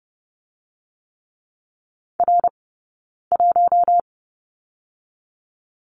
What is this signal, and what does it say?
Morse code, heard as a keyed tone, about 30 wpm, 700 Hz. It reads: R 1
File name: Lesson 20d - Number 3 - Two Characters 30wpm